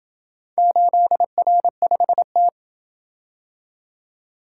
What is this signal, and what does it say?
Morse code, heard as a keyed tone, about 28 wpm, 700 Hz. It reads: 8R5T